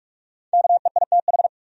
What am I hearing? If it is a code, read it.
Morse code, heard as a keyed tone, about 46 wpm, 700 Hz. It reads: KEITH
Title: USA Common Names - 45wpm